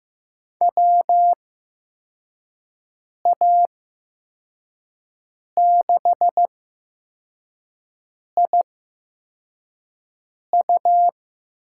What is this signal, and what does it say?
Morse code, heard as a keyed tone, about 15 wpm, spaced as for 4 wpm, 700 Hz. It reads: WA6IU
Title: USA Call Signs - 6 Characters - 8x Character Spacing - 15wpm